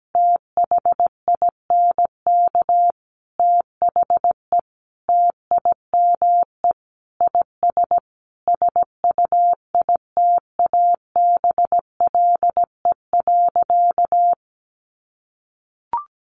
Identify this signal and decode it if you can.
Morse code, heard as a keyed tone, about 17 wpm, 700 Hz. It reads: THINK THE TIME IS SUITABLE.
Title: Sentences from Top 1400 Words 17wpm